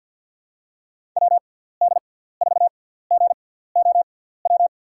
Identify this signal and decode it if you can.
Morse code, heard as a keyed tone, about 50 wpm, spaced as for 17 wpm, 700 Hz. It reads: WD4GOW
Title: USA Call Signs - 6x Character Spacing - 50wpm